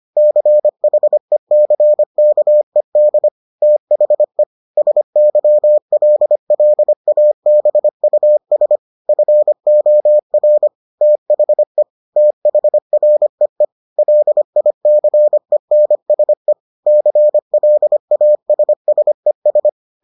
Morse code, heard as a keyed tone, about 25 wpm, 600 Hz. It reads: CHECKED THE SYLLABUS FOR THE THREE LICENSE CLASSES